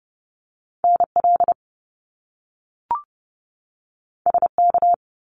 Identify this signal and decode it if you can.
Morse code, heard as a keyed tone, about 30 wpm, 700 Hz. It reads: NL SK
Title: Canadian Provinces and Territories 30wpm